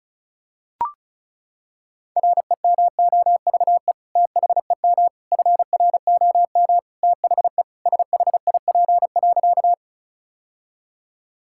Morse code, heard as a keyed tone, about 34 wpm, 700 Hz. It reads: REMOVE THEM FROM THE SHIP.
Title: Sentences from Top 1000 Words - Review Part 5 - 35wpm